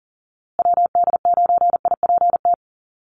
Morse code, heard as a keyed tone, about 40 wpm, 700 Hz. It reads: WD9IPT
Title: USA Call Signs Repeated 3x 40wpm